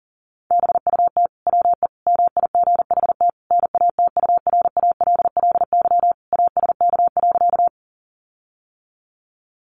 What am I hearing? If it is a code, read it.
Morse code, heard as a keyed tone, about 40 wpm, 700 Hz. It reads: BUT WE MIGHT NATURALLY ASK.
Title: Sentences from Top 1300 Words 40wpm